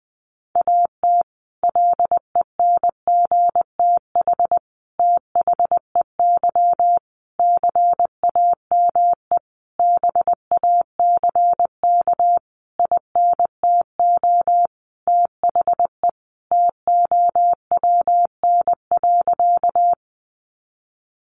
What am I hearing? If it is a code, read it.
Morse code, heard as a keyed tone, about 20 wpm, 700 Hz. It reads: AT LENGTH THEY CAME BACK INTO THE TOWN.
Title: Sentences from Top 1000 Words 20wpm